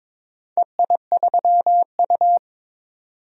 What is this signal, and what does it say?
Morse code, heard as a keyed tone, about 21 wpm, 700 Hz. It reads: EI3U
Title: DXCC-100 Call Signs - 4 Characters - 22wpm